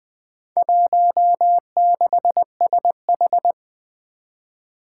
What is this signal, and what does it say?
Morse code, heard as a keyed tone, about 20 wpm, 700 Hz. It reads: 16SH
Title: Four Challenging Letter-Number Combinations 20wpm